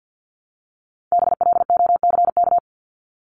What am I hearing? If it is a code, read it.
Morse code, heard as a keyed tone, about 50 wpm, 700 Hz. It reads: BLOCK
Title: Top 1501-1750 Words 50wpm